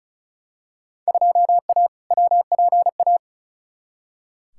Morse code, heard as a keyed tone, about 36 wpm, 700 Hz. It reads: 2A WPA
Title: Field Day Exchanges - 35wpm